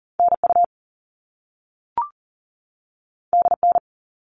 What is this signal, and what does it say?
Morse code, heard as a keyed tone, about 40 wpm, 700 Hz. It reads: NU DN
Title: All Cut Numbers - 2-Digit 40wpm